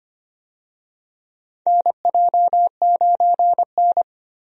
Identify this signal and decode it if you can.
Morse code, heard as a keyed tone, about 25 wpm, 700 Hz. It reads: NJ9N
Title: USA Call Signs - 4 Characters - Repeated 2x - 25wpm